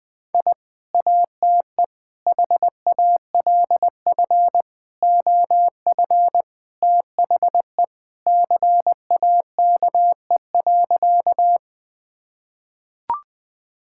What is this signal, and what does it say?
Morse code, heard as a keyed tone, about 21 wpm, 700 Hz. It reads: I ATE HALF OF THE CAKE.
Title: Sentences with 3-Word Phrases - 20wpm